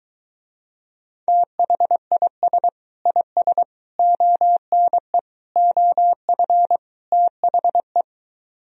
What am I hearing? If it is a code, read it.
Morse code, heard as a keyed tone, about 23 wpm, 700 Hz. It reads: THIS IS ONE OF THE